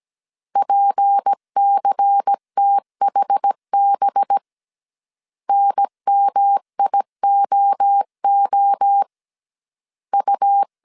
Morse code, heard as a keyed tone, about 17 wpm, 800 Hz. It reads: PCTHB NMIOO U
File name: 16wpm